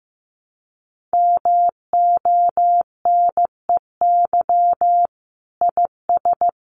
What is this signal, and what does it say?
Morse code, heard as a keyed tone, about 15 wpm, 700 Hz. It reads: MONEY IS
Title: Sentences from Top 500 Words 15wpm